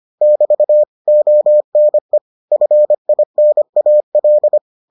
Morse code, heard as a keyed tone, about 25 wpm, 600 Hz. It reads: <BT> ONE FINAL